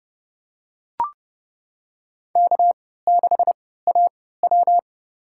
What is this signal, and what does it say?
Morse code, heard as a keyed tone, about 30 wpm, spaced as for 17 wpm, 700 Hz. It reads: K6AW
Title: USA Call Signs - 4 Characters - 3x Character Spacing - 30wpm